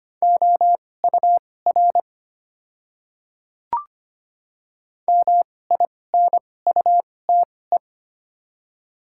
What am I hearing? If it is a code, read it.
Morse code, heard as a keyed tone, about 25 wpm, spaced as for 18 wpm, 700 Hz. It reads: OUR MINUTE
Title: 2x Character Spacing - Top 500 Words - 25wpm